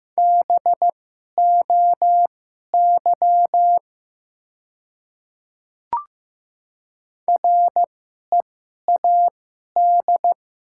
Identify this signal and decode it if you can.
Morse code, heard as a keyed tone, about 15 wpm, spaced as for 11 wpm, 700 Hz. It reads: BOY READ
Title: 2x Character Spacing - Top 500 Words - 15wpm